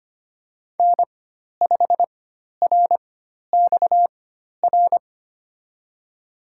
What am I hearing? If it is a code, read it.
Morse code, heard as a keyed tone, about 25 wpm, spaced as for 12 wpm, 700 Hz. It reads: N5RXR
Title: USA Call Signs - 5 Characters - 4x Character Spacing - 25wpm